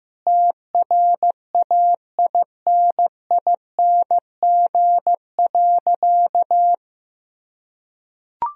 Morse code, heard as a keyed tone, about 15 wpm, 700 Hz. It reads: TRAINING.
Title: Sentences from Top 1000 Words - Review Part 4 - 15wpm